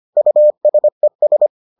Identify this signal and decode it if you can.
Morse code, heard as a keyed tone, about 25 wpm, 600 Hz. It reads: USES